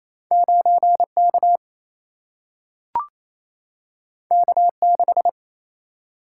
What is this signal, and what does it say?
Morse code, heard as a keyed tone, about 28 wpm, 700 Hz. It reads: 9K K6
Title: Single-Letter Single-Number 28wpm